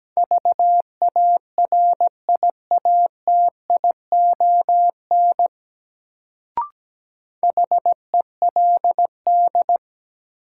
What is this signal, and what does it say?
Morse code, heard as a keyed tone, about 17 wpm, 700 Hz. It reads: VARIATION HELD
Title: Sentences from Top 1600 Words - New Word Drill Alt - 17wpm